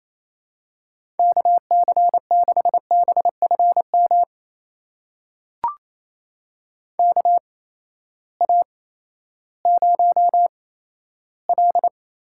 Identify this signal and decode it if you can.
Morse code, heard as a keyed tone, about 28 wpm, spaced as for 14 wpm, 700 Hz. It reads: KC6BFM KA0L
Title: USA Call Signs - 6 Characters - 8x Character Spacing - 28wpm